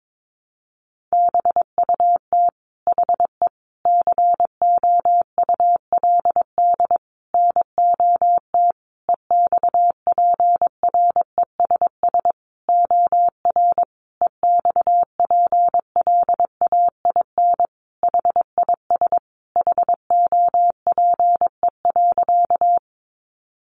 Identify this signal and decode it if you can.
Morse code, heard as a keyed tone, about 22 wpm, 700 Hz. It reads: BUT HE COULD NOT EXPRESS OR EXPLAIN HIS HOPE.
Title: Sentences from Top 1100 Words 22wpm